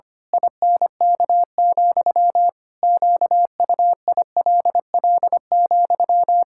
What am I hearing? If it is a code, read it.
Morse code, heard as a keyed tone, about 25 wpm, 700 Hz. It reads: INK, QUILL,
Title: Pangram Challenge v2 - 25wpm